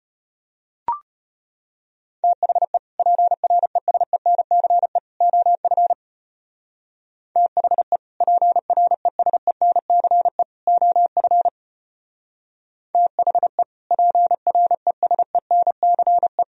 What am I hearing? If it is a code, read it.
Morse code, heard as a keyed tone, about 32 wpm, 700 Hz. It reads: THE PRESENCE OF THE PRESENCE OF THE PRESENCE